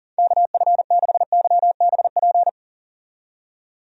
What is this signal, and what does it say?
Morse code, heard as a keyed tone, about 40 wpm, 690 Hz. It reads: KF6YBP
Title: USA Call Signs - 6 Characters - 40wpm